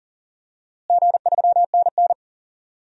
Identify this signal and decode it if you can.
Morse code, heard as a keyed tone, about 40 wpm, 700 Hz. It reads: G3NN
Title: DXCC-100 Call Signs - 4 Characters - Repeated 2x - 40wpm